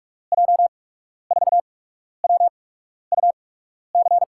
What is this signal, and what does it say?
Morse code, heard as a keyed tone, about 44 wpm, spaced as for 12 wpm, 700 Hz. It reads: J4WUC
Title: USA Call Signs - 6 Characters - 8x Character Spacing - 45wpm